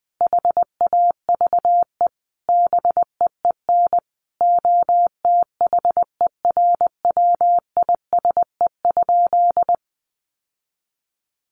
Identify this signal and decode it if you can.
Morse code, heard as a keyed tone, about 20 wpm, 700 Hz. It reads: HAVE BEEN OTHERWISE?
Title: Sentences from Top 1200 Words 20wpm